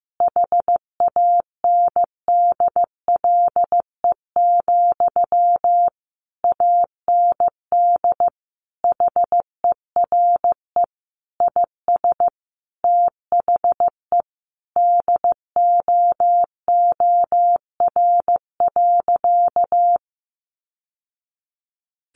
Morse code, heard as a keyed tone, about 15 wpm, 700 Hz. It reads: HANDLE, AND HERE IS THE DOOR.
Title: Sentences from Top 1100 Words 15wpm